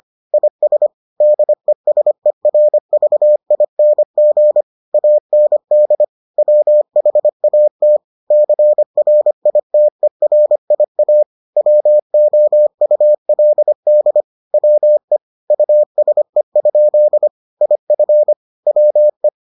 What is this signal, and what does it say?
Morse code, heard as a keyed tone, about 25 wpm, 600 Hz. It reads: IS DESERVING AND WHAT CRITERIA WOULD WE USE? IF WE